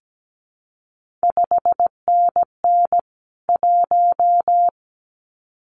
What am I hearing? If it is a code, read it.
Morse code, heard as a keyed tone, about 17 wpm, 700 Hz. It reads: 5NN 1
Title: CQ WW DX Contest Exchange Encore - 17wpm